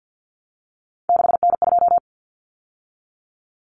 Morse code, heard as a keyed tone, about 50 wpm, 700 Hz. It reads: 6N2
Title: Common Cut Numbers - 3-Digit 50wpm